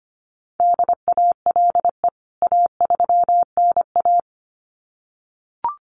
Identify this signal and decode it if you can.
Morse code, heard as a keyed tone, about 25 wpm, 700 Hz. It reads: DALE A3NA
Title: CWT Exchange 25wpm